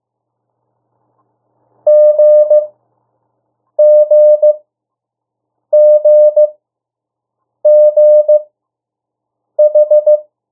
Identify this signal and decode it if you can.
Morse code, heard as a keyed tone, about 15 wpm, spaced as for 6 wpm, 600 Hz. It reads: GGGGH
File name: morse mp3 / 02 urok 1